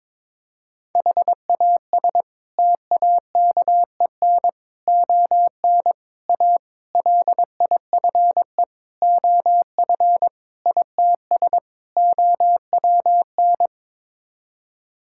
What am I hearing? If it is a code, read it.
Morse code, heard as a keyed tone, about 21 wpm, 700 Hz. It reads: HAS TAKEN ON A LIFE OF ITS OWN
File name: practice / 22wpm / Sets of 8 Words - 22wpm